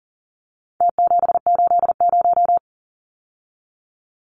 Morse code, heard as a keyed tone, about 40 wpm, 700 Hz. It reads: T780
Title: Numbers - Contest Style - 4-Digit 40wpm